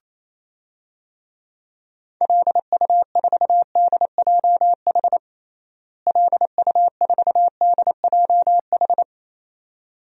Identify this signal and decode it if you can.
Morse code, heard as a keyed tone, about 28 wpm, 700 Hz. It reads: LU4DJH LU4DJH
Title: DXCC-100 Call Signs - 6 Characters - Repeated 2x - 28wpm